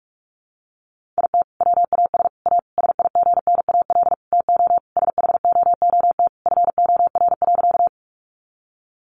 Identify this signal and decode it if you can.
Morse code, heard as a keyed tone, about 44 wpm, 700 Hz. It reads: IT WAS A SIGNAL TO SHOOT FOR.
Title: Sentences from Top 1400 Words 45wpm